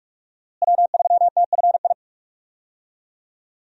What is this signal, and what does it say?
Morse code, heard as a keyed tone, about 45 wpm, 700 Hz. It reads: W3TFI
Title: USA Call Signs - 5 Characters - Repeated 3x - 45wpm